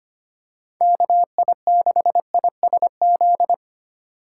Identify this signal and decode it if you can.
Morse code, heard as a keyed tone, about 25 wpm, 700 Hz. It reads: KI6ISZ
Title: USA Call Signs - 6 Characters - 2x Character Spacing - 25wpm